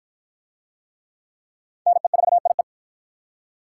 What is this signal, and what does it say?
Morse code, heard as a keyed tone, about 53 wpm, 700 Hz. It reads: NE4IE